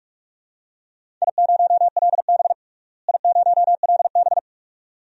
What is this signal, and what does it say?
Morse code, heard as a keyed tone, about 46 wpm, 700 Hz. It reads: I0LB I0LB
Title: DXCC-100 Call Signs - 4 Characters - Repeated 2x - 45wpm